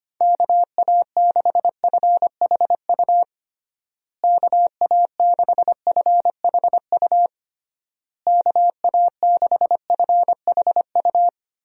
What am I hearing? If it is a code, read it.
Morse code, heard as a keyed tone, about 25 wpm, 700 Hz. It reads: KA6FHU KA6FHU KA6FHU
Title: USA Call Signs - 6 Characters - Repeated 3x - 25wpm